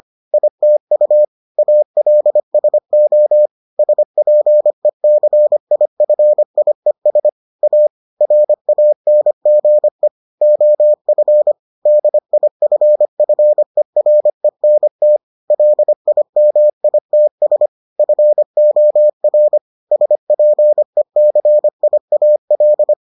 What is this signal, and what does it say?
Morse code, heard as a keyed tone, about 25 wpm, 600 Hz. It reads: ITU ALSO SPECIFIES A RANGE OF DIFFERENT LIMITS FOR SPECIAL